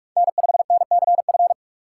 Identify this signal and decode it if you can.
Morse code, heard as a keyed tone, about 46 wpm, 700 Hz. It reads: N5NCF
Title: USA Call Signs - 5 Characters - 2x Character Spacing - 45wpm